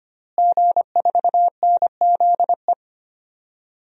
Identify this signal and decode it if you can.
Morse code, heard as a keyed tone, about 25 wpm, 700 Hz. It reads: G4NZE
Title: DXCC-100 Call Signs - 5 Characters - Repeated 2x - 25wpm